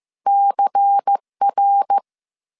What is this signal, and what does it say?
Morse code, heard as a keyed tone, about 15 wpm, 800 Hz. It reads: CR